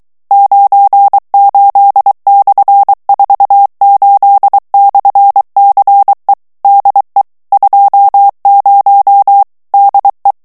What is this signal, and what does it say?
Morse code, heard as a keyed tone, about 23 wpm, 800 Hz. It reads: 98/48/CE DE 20 DE
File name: morse mp3 / ESPAGNOL TEXTE EN CLAIR 2 VITESSE 22 MOTS MINUTES